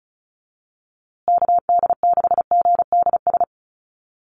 Morse code, heard as a keyed tone, about 34 wpm, 700 Hz. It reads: KD6GDS
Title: USA Call Signs Repeated 3x 35wpm